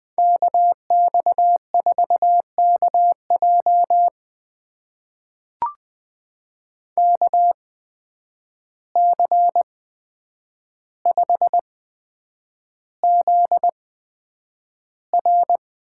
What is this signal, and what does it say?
Morse code, heard as a keyed tone, about 20 wpm, spaced as for 9 wpm, 700 Hz. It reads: KX4KJ KC5ZR
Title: USA Call Signs - 5 Characters - 8x Character Spacing - 20wpm